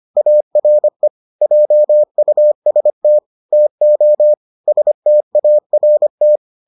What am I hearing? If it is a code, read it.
Morse code, heard as a keyed tone, about 25 wpm, 600 Hz. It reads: ARE JUST TO START